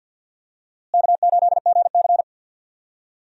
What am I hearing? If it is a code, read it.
Morse code, heard as a keyed tone, about 51 wpm, 700 Hz. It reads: K8GC
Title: USA Call Signs - 4 Characters - Repeated 3x - 50wpm